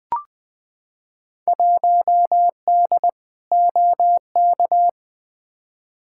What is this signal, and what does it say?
Morse code, heard as a keyed tone, about 20 wpm, 700 Hz. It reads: 1D OK
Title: Field Day Exchanges - 20wpm